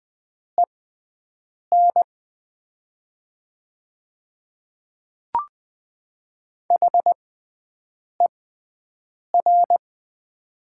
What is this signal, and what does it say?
Morse code, heard as a keyed tone, about 20 wpm, spaced as for 7 wpm, 700 Hz. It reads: EN HER
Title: 6x Character Spacing - Top 100 Words - 20wpm